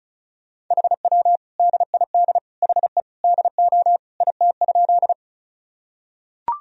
Morse code, heard as a keyed tone, about 36 wpm, 700 Hz. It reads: HW DID HE DO IT?
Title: Sentences from Top 100 Words and Abbreviations 35wpm